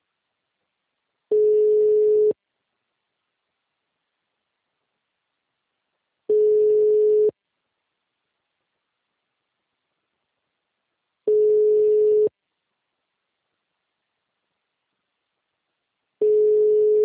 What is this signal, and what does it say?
Ringback tone (Europe, 425 Hz).
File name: out- -2011- - - .80